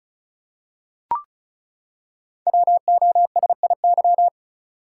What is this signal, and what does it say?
Morse code, heard as a keyed tone, about 36 wpm, 700 Hz. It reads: WOSIY
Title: Code Groups - 5 Letter:Numbers - 35wpm